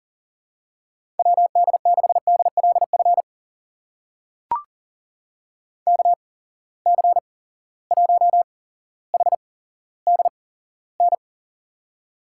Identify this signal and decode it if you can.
Morse code, heard as a keyed tone, about 40 wpm, spaced as for 17 wpm, 700 Hz. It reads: WD6DLF KC1HDN
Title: USA Call Signs - 6 Characters - 8x Character Spacing - 40wpm